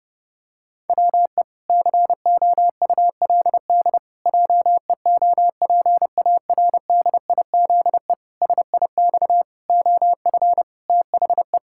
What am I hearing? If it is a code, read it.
Morse code, heard as a keyed tone, about 30 wpm, 700 Hz. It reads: WE COULD JEOPARDIZE SIX OF THE
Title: Pangram Challenge v2 - 30wpm